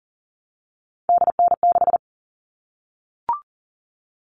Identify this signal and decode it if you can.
Morse code, heard as a keyed tone, about 40 wpm, 700 Hz. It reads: DN6
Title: All Cut Numbers - 3-Digit 40wpm